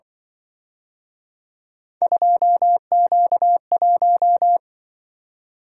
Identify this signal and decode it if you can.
Morse code, heard as a keyed tone, about 25 wpm, 700 Hz. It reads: 2Q1
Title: Speed-Racing - Three Letter-Number Combinations - 20wpm